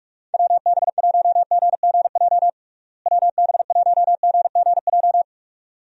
Morse code, heard as a keyed tone, about 46 wpm, 700 Hz. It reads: WB1GGJ WB1GGJ
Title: USA Call Signs - 6 Characters - Repeated 2x - 45wpm